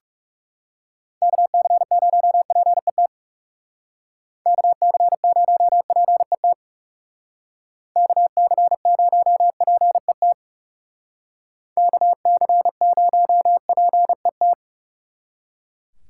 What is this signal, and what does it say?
Morse code, heard as a keyed tone, about 35 wpm, 700 Hz. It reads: KC0PET KC0PET KC0PET KC0PET
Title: Speed Racing - USA 2x3 Call Signs 30wpm